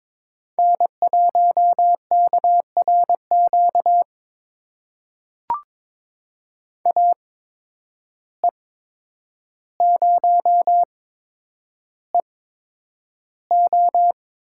Morse code, heard as a keyed tone, about 21 wpm, spaced as for 9 wpm, 700 Hz. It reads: N1KRQ AE0EO
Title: USA Call Signs - 5 Characters - 8x Character Spacing - 22wpm